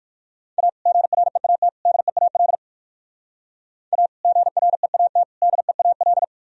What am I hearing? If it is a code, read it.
Morse code, heard as a keyed tone, about 57 wpm, 700 Hz. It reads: A GREAT DEAL A GREAT DEAL